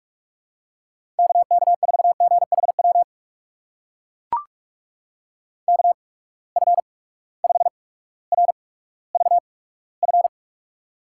Morse code, heard as a keyed tone, about 44 wpm, spaced as for 19 wpm, 700 Hz. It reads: KK4GHW KF5RVF
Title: USA Call Signs - 6 Characters - 8x Character Spacing - 45wpm